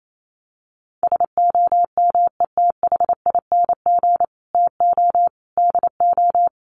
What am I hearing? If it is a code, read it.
Morse code, heard as a keyed tone, about 28 wpm, 700 Hz. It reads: SOMETHING TO DO
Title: Sets of 5 Words Encore 28wpm